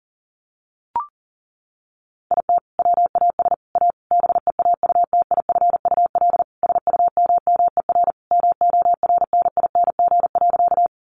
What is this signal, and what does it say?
Morse code, heard as a keyed tone, about 40 wpm, 700 Hz. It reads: IT WAS A BEAUTIFUL SUMMER MORNING.
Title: Sentences from Top 1100 Words 40wpm